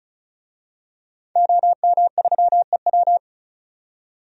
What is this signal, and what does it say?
Morse code, heard as a keyed tone, about 34 wpm, 700 Hz. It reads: OM3EW
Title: DXCC-100 Call Signs - 5 Characters - 35wpm